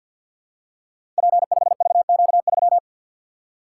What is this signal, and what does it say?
Morse code, heard as a keyed tone, about 50 wpm, 700 Hz. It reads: P5VQ3
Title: Code Groups - 5 Letter:Numbers - 50wpm